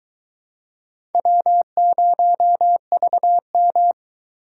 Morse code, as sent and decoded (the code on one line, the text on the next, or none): .-- ----- ...- --
W0VM